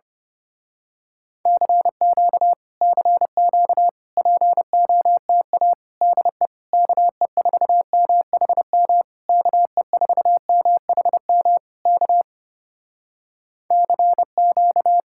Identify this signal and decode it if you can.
Morse code, heard as a keyed tone, about 27 wpm, 700 Hz. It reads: CQ CQ POTA DE KE4MHM KE4MHM K CQ
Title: Speed-Racing Common QSO Phrases Encore - 25wpm